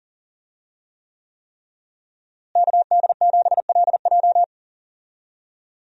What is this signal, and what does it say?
Morse code, heard as a keyed tone, about 40 wpm, 700 Hz. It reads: KD7LJ
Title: USA Call Signs - 5 Characters - Repeated 3x - 40wpm